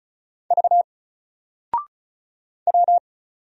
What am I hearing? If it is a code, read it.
Morse code, heard as a keyed tone, about 36 wpm, 700 Hz. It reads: V W
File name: Single-Letter-Number-35wpm